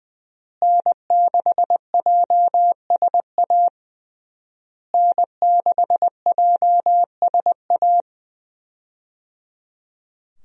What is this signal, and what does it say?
Morse code, heard as a keyed tone, about 20 wpm, 700 Hz. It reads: N6JSA N6JSA
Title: USA Call Signs - 5 Characters - Repeated 3x - 20wpm